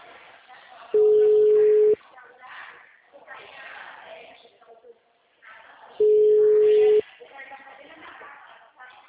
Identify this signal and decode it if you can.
Ringback tone (Europe, 425 Hz), with speech.
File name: out- -2029- - - .38075